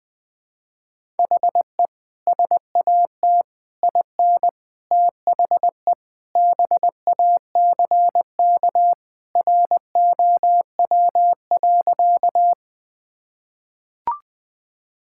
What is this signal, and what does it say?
Morse code, heard as a keyed tone, about 20 wpm, 700 Hz. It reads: HE SAT IN THE BACK ROW.
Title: Sentences with 3-Word Phrases - 20wpm